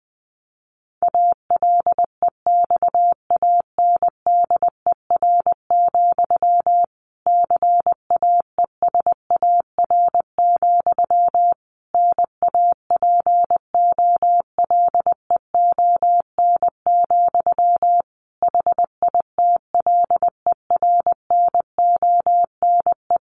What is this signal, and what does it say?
Morse code, heard as a keyed tone, about 20 wpm, 700 Hz. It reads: ALEXANDER, CAESAR, NAPOLEON, HITLER NONE
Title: A Sound Of Thunder 20wpm